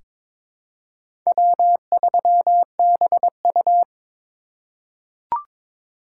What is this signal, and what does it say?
Morse code, heard as a keyed tone, about 22 wpm, 700 Hz. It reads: W3BU
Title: USA Call Signs - 4 Characters - 3x Character Spacing - 22wpm